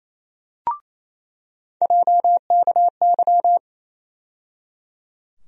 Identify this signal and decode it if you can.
Morse code, heard as a keyed tone, about 28 wpm, 700 Hz. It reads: JKY
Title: USA Call Sign Postfixes - 3 Letters 28wpm